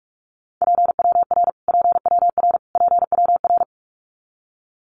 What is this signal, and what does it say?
Morse code, heard as a keyed tone, about 44 wpm, 700 Hz. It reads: PWR PWR PWR
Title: Abbreviations Repeated 3x - 45wpm